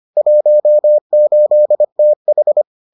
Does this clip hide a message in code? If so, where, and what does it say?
Yes: Morse code from 0.2 to 2.6 s, heard as a keyed tone, about 25 wpm, 600 Hz. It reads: 18TH